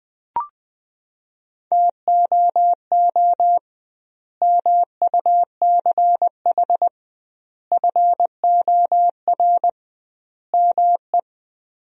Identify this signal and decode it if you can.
Morse code, heard as a keyed tone, about 21 wpm, 700 Hz. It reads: TOO MUCH FOR ME
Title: 2x Word-Spacing - Sentences from Top 200 Words - 20wpm